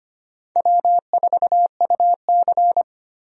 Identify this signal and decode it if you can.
Morse code, heard as a keyed tone, about 25 wpm, 700 Hz. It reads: W4UC
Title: USA Call Signs - 4 Characters - 25wpm